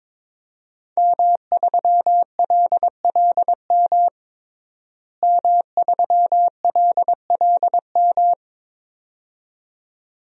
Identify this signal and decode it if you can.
Morse code, heard as a keyed tone, about 22 wpm, 700 Hz. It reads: M3LLM M3LLM
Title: DXCC-100 Call Signs - 5 Characters - Repeated 2x - 22wpm